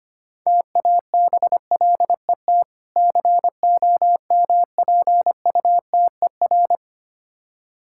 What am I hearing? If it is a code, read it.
Morse code, heard as a keyed tone, about 26 wpm, 700 Hz. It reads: TABLET COMPUTER
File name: practice / 25wpm / Difficult Words - Speed-Racing with Hints - 25wpm